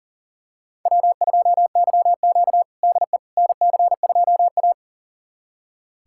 Morse code, heard as a keyed tone, about 40 wpm, 700 Hz. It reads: W2YQ DE NC2A
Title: USA Callsign DE Callsign - 4 Characters - 40wpm